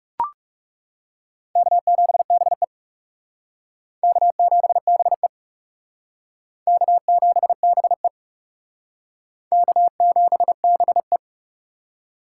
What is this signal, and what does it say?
Morse code, heard as a keyed tone, about 40 wpm, 700 Hz. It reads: K7BE K7BE K7BE K7BE